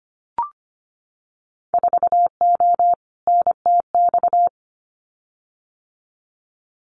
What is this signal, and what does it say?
Morse code, heard as a keyed tone, about 25 wpm, 700 Hz. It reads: 4O NTX
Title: Winter Field Day Exchanges 25wpm